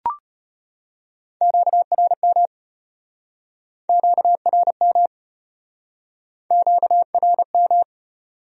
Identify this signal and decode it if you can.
Morse code, heard as a keyed tone, about 38 wpm, 700 Hz. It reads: QRM QRM QRM